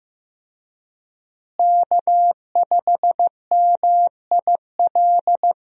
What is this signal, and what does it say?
Morse code, heard as a keyed tone, about 15 wpm, 700 Hz. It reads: K5MIL
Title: USA Call Signs - 5 Characters - Repeated 2x - 15wpm